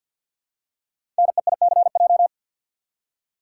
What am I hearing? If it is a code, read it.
Morse code, heard as a keyed tone, about 48 wpm, 690 Hz. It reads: NEICJ
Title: Code Groups - 5 Letters - 50wpm